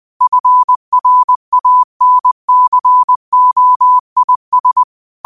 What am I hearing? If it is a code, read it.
Morse code, heard as a keyed tone, about 20 wpm, 1000 Hz. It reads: FRANCOIS